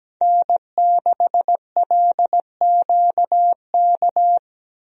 Morse code, heard as a keyed tone, about 17 wpm, 700 Hz. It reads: N6LQK
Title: USA Call Signs - 5 Characters - 17wpm